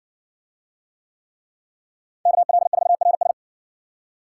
Morse code, heard as a keyed tone, about 60 wpm, 700 Hz. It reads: KL4RS